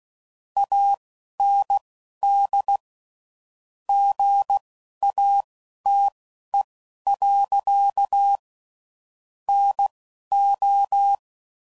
Morse code, heard as a keyed tone, about 16 wpm, spaced as for 11 wpm, 800 Hz. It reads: AND GATE. NO
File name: CW Code Groups @10WPM / Technical Subjects / AND Gate